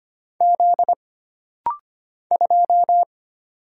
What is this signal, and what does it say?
Morse code, heard as a keyed tone, about 25 wpm, 700 Hz. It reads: Z 2
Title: Single Letter-Number Alt - Rapid-Fire 25wpm